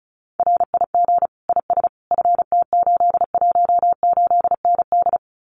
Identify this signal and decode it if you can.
Morse code, heard as a keyed tone, about 34 wpm, 700 Hz. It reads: RIG IS FT818ND
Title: Common QSO Phrases Encore 35wpm